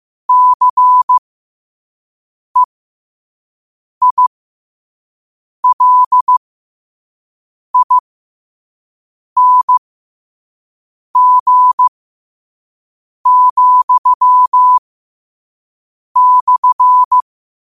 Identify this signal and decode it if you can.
Morse code, heard as a keyed tone, about 15 wpm, spaced as for 5 wpm, 1000 Hz. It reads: CEILING,/